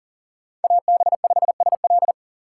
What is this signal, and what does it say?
Morse code, heard as a keyed tone, about 37 wpm, 700 Hz. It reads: AB5SL
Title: USA Call Signs - 5 Characters - 3x Character Spacing - 40wpm